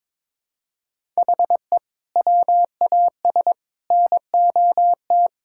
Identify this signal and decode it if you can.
Morse code, heard as a keyed tone, about 22 wpm, 700 Hz. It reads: HE WAS NOT
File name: practice / 22wpm / Sentences with 4-Word Phrases - 22wpm